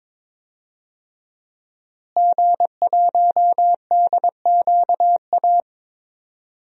Morse code, heard as a keyed tone, about 22 wpm, 700 Hz. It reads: G1DQA
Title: DXCC-100 Call Signs - 5 Characters - Repeated 2x - 22wpm